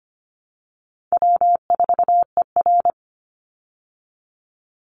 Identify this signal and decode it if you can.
Morse code, heard as a keyed tone, about 25 wpm, 700 Hz. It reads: W4ER
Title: Call Signs Repeated 3x - 25wpm